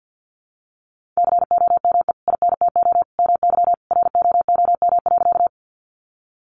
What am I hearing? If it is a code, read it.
Morse code, heard as a keyed tone, about 51 wpm, 700 Hz. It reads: COME INTO MY ROOM.
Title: Sentences from Top 600 Words 50wpm